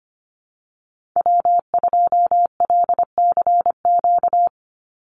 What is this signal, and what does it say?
Morse code, heard as a keyed tone, about 25 wpm, 700 Hz. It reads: W2LCQ
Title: Call Signs - 25wpm